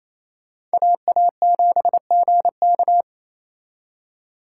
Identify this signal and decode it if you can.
Morse code, heard as a keyed tone, about 35 wpm, 700 Hz. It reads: AA7GK